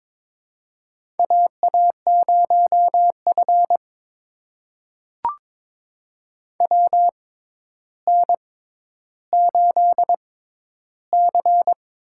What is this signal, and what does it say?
Morse code, heard as a keyed tone, about 22 wpm, spaced as for 11 wpm, 700 Hz. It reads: AA0F WN8C
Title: USA Call Signs - 4 Characters - 6x Character Spacing - 22wpm